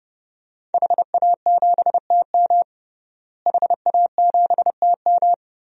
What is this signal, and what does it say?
Morse code, heard as a keyed tone, about 30 wpm, 700 Hz. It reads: HA7TM HA7TM
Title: DXCC-100 Call Signs - 5 Characters - Repeated 2x - 30wpm